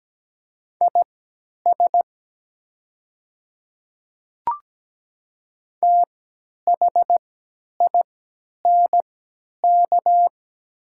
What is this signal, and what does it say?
Morse code, heard as a keyed tone, about 17 wpm, spaced as for 10 wpm, 700 Hz. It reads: IS THINK
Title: 3x Character Spacing - Top 200 Words - 17wpm